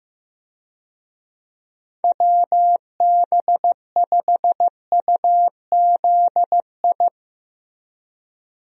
Morse code, heard as a keyed tone, about 15 wpm, 700 Hz. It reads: WB5UZI
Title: USA Call Signs - 6 Characters - Repeated 3x - 15wpm